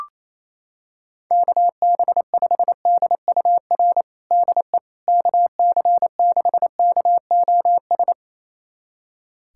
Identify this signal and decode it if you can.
Morse code, heard as a keyed tone, about 28 wpm, 700 Hz. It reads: KB5DUR DE KC6KOS
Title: USA Callsign DE Callsign - 6 Characters - 28wpm